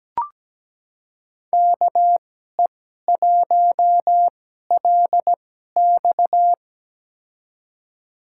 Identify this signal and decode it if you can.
Morse code, heard as a keyed tone, about 17 wpm, spaced as for 12 wpm, 700 Hz. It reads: KE1LX
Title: USA Call Signs - 2x Character Spacing - 17wpm